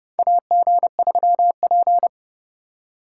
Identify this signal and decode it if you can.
Morse code, heard as a keyed tone, about 29 wpm, 700 Hz. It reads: AG3P
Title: USA Call Signs - 4 Characters - 4x Character Spacing - 30wpm